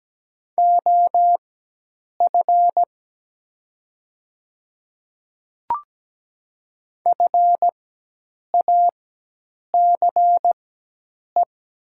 Morse code, heard as a keyed tone, about 17 wpm, spaced as for 8 wpm, 700 Hz. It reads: OF FACE
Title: 4x Character Spacing - Top 300 Words - 17wpm